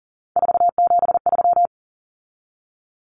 Morse code, heard as a keyed tone, about 40 wpm, 700 Hz. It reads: 473
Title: Numbers - 3-Digit 40wpm